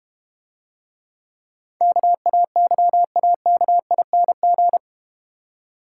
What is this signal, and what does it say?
Morse code, heard as a keyed tone, about 33 wpm, 700 Hz. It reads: KAYAKING